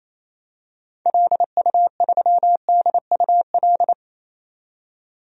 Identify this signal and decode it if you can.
Morse code, heard as a keyed tone, about 27 wpm, 700 Hz. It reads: LU3DUL
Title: DXCC-100 Call Signs - 6 Characters - 28wpm